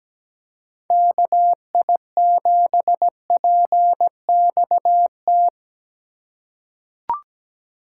Morse code, heard as a keyed tone, about 17 wpm, 700 Hz. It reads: KI7PXT
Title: USA Call Signs - 6 Characters - 4x Character Spacing - 17wpm